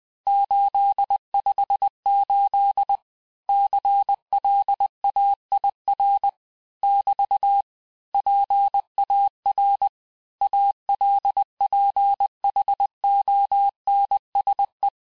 Morse code, heard as a keyed tone, about 20 wpm, 800 Hz. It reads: 858 CLAIR <BT> PAR ALPHONSE
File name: cours-du-mercredi / audios / cours-858-texte-clais-vitesse-19mpm